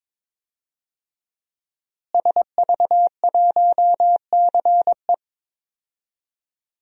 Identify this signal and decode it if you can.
Morse code, heard as a keyed tone, about 22 wpm, 700 Hz. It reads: SV1CE
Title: DXCC-100 Call Signs - 5 Characters - Repeated 2x - 22wpm